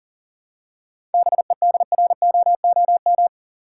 Morse code, heard as a keyed tone, about 40 wpm, 700 Hz. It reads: BEDROOM
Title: Seven Letter Words - 40wpm